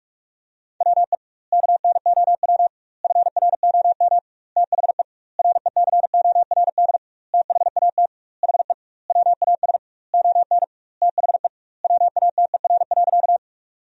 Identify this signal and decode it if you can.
Morse code, heard as a keyed tone, about 46 wpm, 700 Hz. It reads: WE KNOW FROM THE RECORD THAT HE WAS ON THE WATER.
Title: 2x Word-Spacing - Sentences from Top 500 Words - 45wpm